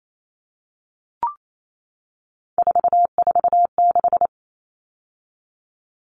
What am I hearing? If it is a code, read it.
Morse code, heard as a keyed tone, about 28 wpm, 700 Hz. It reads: 446
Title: Numbers - 3-Digit 28wpm